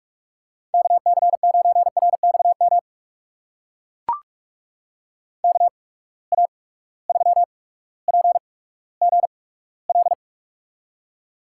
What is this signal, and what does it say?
Morse code, heard as a keyed tone, about 44 wpm, spaced as for 19 wpm, 700 Hz. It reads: KC9RXM KA3PGL
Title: USA Call Signs - 6 Characters - 8x Character Spacing - 45wpm